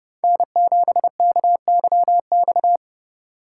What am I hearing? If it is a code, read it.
Morse code, heard as a keyed tone, about 36 wpm, 700 Hz. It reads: N7KYX
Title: Speed Racing - USA Call Signs 30wpm